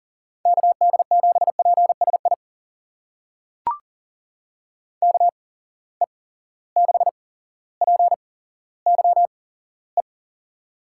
Morse code, heard as a keyed tone, about 40 wpm, spaced as for 17 wpm, 700 Hz. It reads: KD7PSI KE6PYE
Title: USA Call Signs - 6 Characters - 8x Character Spacing - 40wpm